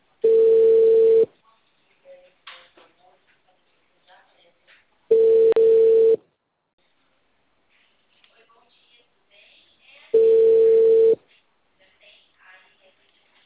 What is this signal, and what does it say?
Ringback tone (China, 450 Hz), with speech.